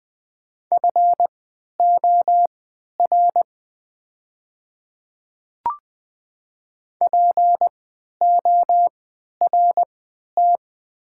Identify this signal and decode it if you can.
Morse code, heard as a keyed tone, about 20 wpm, spaced as for 11 wpm, 700 Hz. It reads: FOR PORT
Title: 3x Character Spacing - Top 300 Words - 20wpm